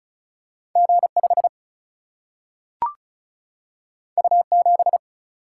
Morse code, heard as a keyed tone, about 36 wpm, 700 Hz. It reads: G 5 U 7
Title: Single-Letter Single-Number 35wpm